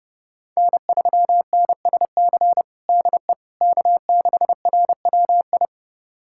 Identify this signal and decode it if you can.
Morse code, heard as a keyed tone, about 30 wpm, 700 Hz. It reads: N3NSC DE K6RWI
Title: USA Callsign DE Callsign - 5 Characters - 30wpm